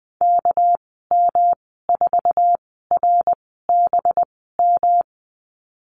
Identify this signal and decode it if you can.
Morse code, heard as a keyed tone, about 20 wpm, spaced as for 14 wpm, 700 Hz. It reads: KM4RBM
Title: 245 Lesson 39g - Letter z - Call Signs 20wpm